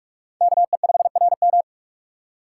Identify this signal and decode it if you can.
Morse code, heard as a keyed tone, about 40 wpm, 700 Hz. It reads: KE5RM